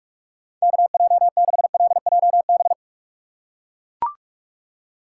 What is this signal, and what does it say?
Morse code, heard as a keyed tone, about 41 wpm, 700 Hz. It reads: KJ6LJB
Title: USA Call Signs - 6 Characters - 4x Character Spacing - 45wpm